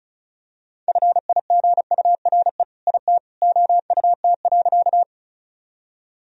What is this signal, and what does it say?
Morse code, heard as a keyed tone, about 36 wpm, 700 Hz. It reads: FIGURE IT OUT.
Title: Sentences from Top 1000 Words - Review Part 2 - 35wpm